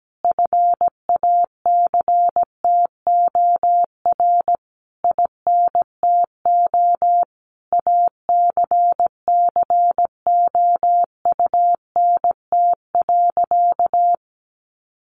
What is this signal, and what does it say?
Morse code, heard as a keyed tone, about 17 wpm, 700 Hz. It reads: FACTOR INTO ACCOUNT.
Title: Sentences from Top 1300 Words 17wpm